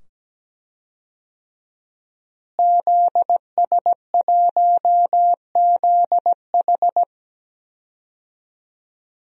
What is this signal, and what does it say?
Morse code, heard as a keyed tone, about 17 wpm, 700 Hz. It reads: ZS1ZH
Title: DXCC-100 Call Signs - 5 Characters - Repeated 2x - 17wpm